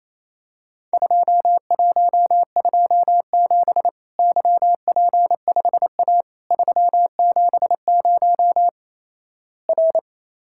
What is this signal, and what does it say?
Morse code, heard as a keyed tone, about 28 wpm, 700 Hz. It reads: 2127 YP5A 370 R
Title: Worked All Europe DX Contest - QTC Exchange - 28wpm